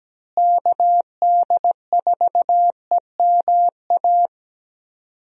Morse code, heard as a keyed tone, about 17 wpm, 700 Hz. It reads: KD4EMA
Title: USA Call Signs - 6 Characters - Repeated 2x - 17wpm